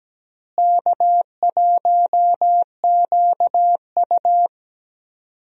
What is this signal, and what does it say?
Morse code, heard as a keyed tone, about 17 wpm, 700 Hz. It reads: K1QU
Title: USA Call Signs - 4 Characters - Repeated 3x - 17wpm